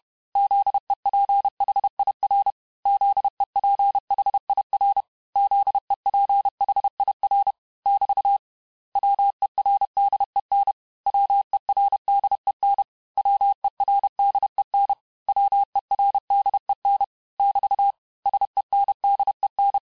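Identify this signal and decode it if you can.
Morse code, heard as a keyed tone, about 31 wpm, 800 Hz. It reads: ZEPHIR ZEPHIR ZEPHIR <BT> WERDEN WERDEN WERDEN WERDEN <BT> SENDEN
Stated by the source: Morse lesson of words, each word repeated 4 times, at 29 words per minute.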